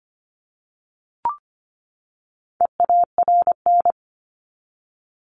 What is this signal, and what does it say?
Morse code, heard as a keyed tone, about 25 wpm, 700 Hz. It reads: EARN
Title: Top 1251-1500 Words 25wpm